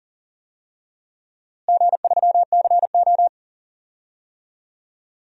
Morse code, heard as a keyed tone, about 40 wpm, 700 Hz. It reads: G3CO
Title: DXCC-100 Call Signs - 4 Characters - Repeated 2x - 40wpm